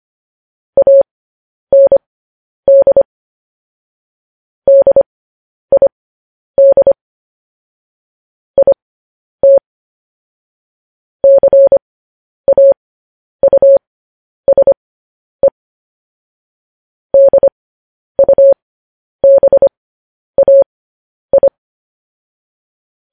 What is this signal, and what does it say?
Morse code, heard as a keyed tone, about 25 wpm, spaced as for 10 wpm, 550 Hz. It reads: AND DID IT CAUSE DUBAI
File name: podcast / News headlines-1x-25x10